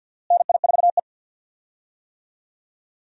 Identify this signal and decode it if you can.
Morse code, heard as a keyed tone, about 47 wpm, 700 Hz. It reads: NI4E
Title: USA Call Signs - 4 Characters - Repeated 2x - 50wpm